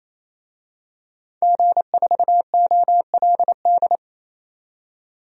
Morse code, heard as a keyed tone, about 28 wpm, 700 Hz. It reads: G4OLD
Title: DXCC-100 Call Signs - 5 Characters - 28wpm